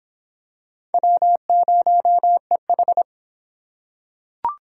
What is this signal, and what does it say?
Morse code, heard as a keyed tone, about 23 wpm, 700 Hz. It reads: W0EH